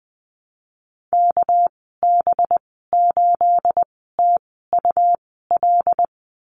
Morse code, heard as a keyed tone, about 20 wpm, spaced as for 14 wpm, 700 Hz. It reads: KB8TUL
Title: 245 Lesson 39g - Letter z - Call Signs 20wpm